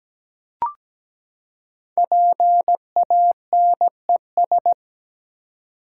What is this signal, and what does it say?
Morse code, heard as a keyed tone, about 17 wpm, 700 Hz. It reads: PANES
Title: Code Groups - 5 Letters - 17wpm